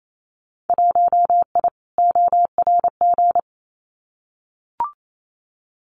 Morse code, heard as a keyed tone, about 28 wpm, 700 Hz. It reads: 1I ORG
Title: Winter Field Day Exchanges 28wpm